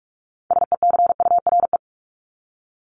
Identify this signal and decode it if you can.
Morse code, heard as a keyed tone, about 44 wpm, 700 Hz. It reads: SECURE
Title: Top 1001-1250 Words 45wpm